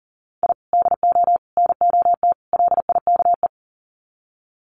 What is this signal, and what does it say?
Morse code, heard as a keyed tone, about 40 wpm, 700 Hz. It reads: I DO NOT LIKE
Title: Sets of 4 Words 40wpm